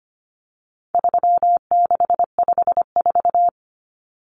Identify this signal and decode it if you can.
Morse code, heard as a keyed tone, about 25 wpm, 700 Hz. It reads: 3654
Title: Common Cut Numbers - 4-Digit 25wpm